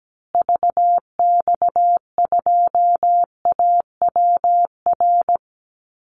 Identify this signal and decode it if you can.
Morse code, heard as a keyed tone, about 17 wpm, 700 Hz. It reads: VX2AWR
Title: Call Signs Repeated 2x - 17wpm